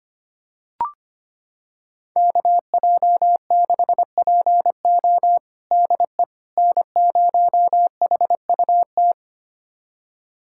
Morse code, heard as a keyed tone, about 25 wpm, 700 Hz. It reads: KJ6PO DE N0HUT
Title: USA Callsign DE Callsign - 5 Characters - 25wpm